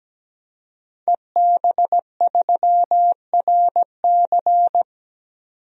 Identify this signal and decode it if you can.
Morse code, heard as a keyed tone, about 17 wpm, 700 Hz. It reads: EB3RC
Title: DXCC-100 Call Signs - 5 Characters - 17wpm